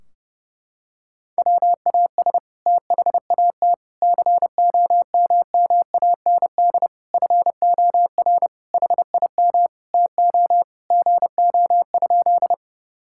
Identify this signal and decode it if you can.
Morse code, heard as a keyed tone, about 30 wpm, 700 Hz. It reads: WAS THAT COMMAND FOR HIM TO GO?
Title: Sentences from Top 1600 Words 30wpm